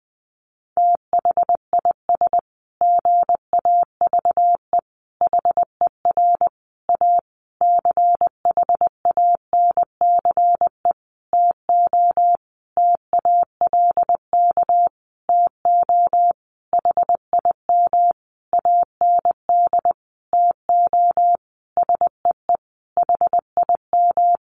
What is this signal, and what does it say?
Morse code, heard as a keyed tone, about 20 wpm, 700 Hz. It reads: THIS GAVE HER A CHANCE TO TALK TO HIM AND TO SEE HIM
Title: Sentences from Top 1400 Words 20wpm